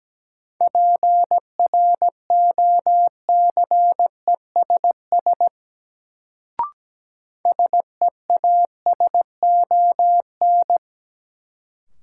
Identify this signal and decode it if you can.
Morse code, heard as a keyed tone, about 17 wpm, 700 Hz. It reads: PROCESS SEASON
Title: Sentences from Top 1400 Words - New Word Drill Alt - 17wpm